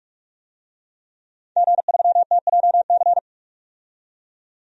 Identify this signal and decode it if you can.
Morse code, heard as a keyed tone, about 45 wpm, 700 Hz. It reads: G3TJC
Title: DXCC-100 Call Signs - 5 Characters - 45wpm